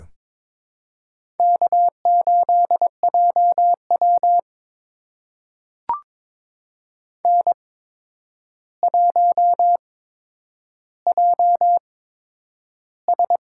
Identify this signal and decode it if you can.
Morse code, heard as a keyed tone, about 22 wpm, spaced as for 9 wpm, 700 Hz. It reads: K8JW N1JS
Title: USA Call Signs - 4 Characters - 8x Character Spacing - 22wpm